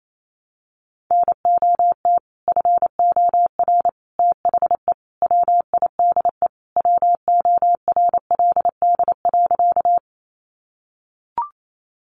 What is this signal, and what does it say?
Morse code, heard as a keyed tone, about 28 wpm, 700 Hz. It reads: NOT FOR THE WIDE WORLD.
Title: Sentences from Top 900 Words 28wpm